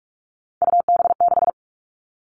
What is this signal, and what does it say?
Morse code, heard as a keyed tone, about 44 wpm, 700 Hz. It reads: UB6
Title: All Cut Numbers - 3-Digit 45wpm